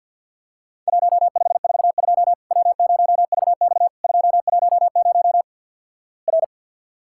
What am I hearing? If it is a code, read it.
Morse code, heard as a keyed tone, about 50 wpm, 700 Hz. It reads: 1542 W0VX 210 R
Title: Worked All Europe DX Contest - QTC Exchange - 50wpm